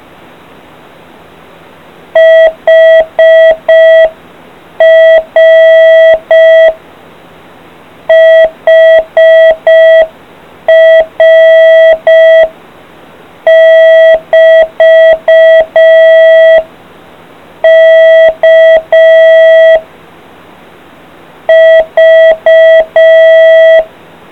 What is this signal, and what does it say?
Morse code, heard as a keyed tone, about 5 wpm, 650 Hz. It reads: HR HR <BT> K V